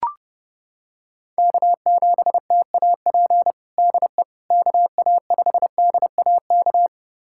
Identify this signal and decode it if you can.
Morse code, heard as a keyed tone, about 32 wpm, 700 Hz. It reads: K7TAP DE KA5DAK
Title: USA Callsign DE Callsign 30wpm